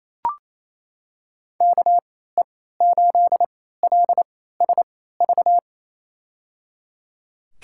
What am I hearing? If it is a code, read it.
Morse code, heard as a keyed tone, about 28 wpm, spaced as for 16 wpm, 700 Hz. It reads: KE8LSV
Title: USA Call Signs - 3x Character Spacing - 28wpm